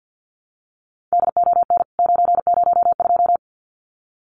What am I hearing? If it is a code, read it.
Morse code, heard as a keyed tone, about 48 wpm, 700 Hz. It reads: DON 902
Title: CWT Exchange 50wpm